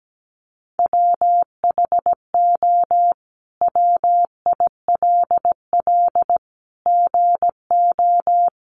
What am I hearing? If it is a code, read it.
Morse code, heard as a keyed tone, about 17 wpm, 700 Hz. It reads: WHO WILL GO
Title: Sentences from Top 100 Words 17wpm